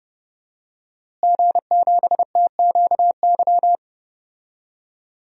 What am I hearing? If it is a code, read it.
Morse code, heard as a keyed tone, about 32 wpm, 700 Hz. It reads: G7TQY
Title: DXCC-100 Call Signs - 5 Characters - Repeated 3x - 30wpm